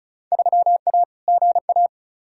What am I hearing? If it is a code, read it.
Morse code, heard as a keyed tone, about 34 wpm, 700 Hz. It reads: 3A GA
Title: Field Day Exchanges Encore - 35wpm